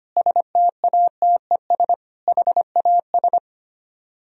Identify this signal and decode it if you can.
Morse code, heard as a keyed tone, about 32 wpm, 700 Hz. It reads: STATES HAS